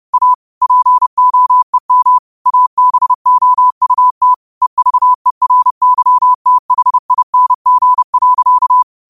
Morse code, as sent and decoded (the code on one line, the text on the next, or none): .- / .--. --- . -- / .- -... --- ..- - / . ...- . .-. -.-- - .... .. -. --. .-.-.-
A POEM ABOUT EVERYTHING.